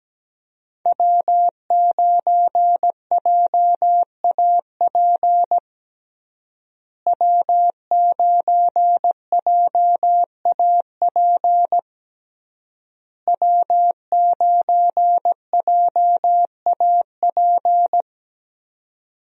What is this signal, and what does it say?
Morse code, heard as a keyed tone, about 17 wpm, 700 Hz. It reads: W9JAP W9JAP W9JAP
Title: USA Call Signs - 5 Characters - Repeated 3x - 17wpm